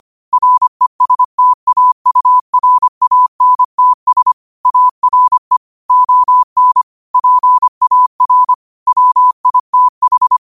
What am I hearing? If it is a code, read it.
Morse code, heard as a keyed tone, about 25 wpm, 1000 Hz. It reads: RESTAURANTS ARE ON PAR WITH
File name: qotdfiles / 25wpm